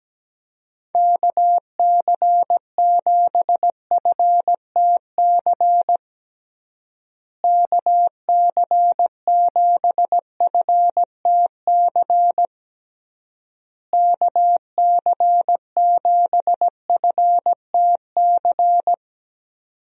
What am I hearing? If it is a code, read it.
Morse code, heard as a keyed tone, about 17 wpm, 700 Hz. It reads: KC7FTC KC7FTC KC7FTC
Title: USA Call Signs - 6 Characters - Repeated 3x - 17wpm